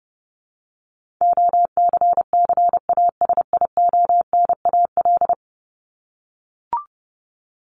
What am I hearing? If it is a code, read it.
Morse code, heard as a keyed tone, about 30 wpm, 700 Hz. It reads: OCCASIONAL
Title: LY Suffix - Top 100 Words with Base Words 30wpm